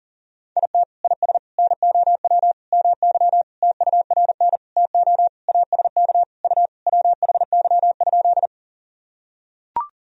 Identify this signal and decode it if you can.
Morse code, heard as a keyed tone, about 40 wpm, 700 Hz. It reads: IT IS NOW MY TURN TO ASK U WHY?
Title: Sentences from Top 200 Words and Abbreviations 40wpm